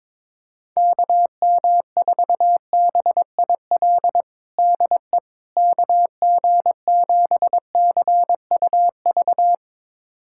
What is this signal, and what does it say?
Morse code, heard as a keyed tone, about 22 wpm, 700 Hz. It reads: KM4BIL DE KG7CUV
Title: USA Callsign DE Callsign - 6 Characters - 22wpm